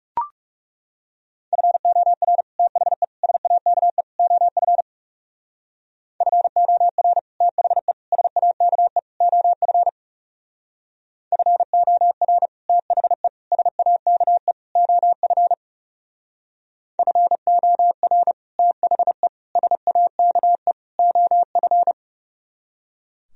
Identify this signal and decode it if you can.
Morse code, heard as a keyed tone, about 36 wpm, 700 Hz. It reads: FOR THE SAKE OF FOR THE SAKE OF FOR THE SAKE OF FOR THE SAKE OF
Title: Speed-Racing - Sets of 4 Words 30wpm